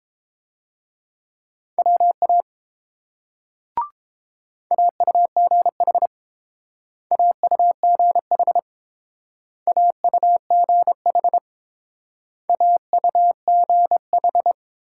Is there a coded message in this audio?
Yes: Morse code from 1.8 to 14.5 s, heard as a keyed tone, about 29 wpm, 700 Hz. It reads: WA AUGH AUGH AUGH AUGH